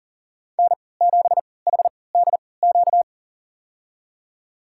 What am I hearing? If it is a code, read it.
Morse code, heard as a keyed tone, about 40 wpm, spaced as for 23 wpm, 700 Hz. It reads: N7HDQ
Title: USA Call Signs - 5 Characters - 3x Character Spacing - 40wpm